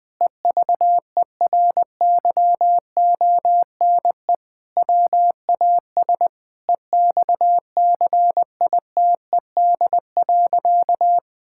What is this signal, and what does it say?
Morse code, heard as a keyed tone, about 21 wpm, 700 Hz. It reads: EVERYONE WAS EXCITED.
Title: Sentences with 5-Word Phrases - 20wpm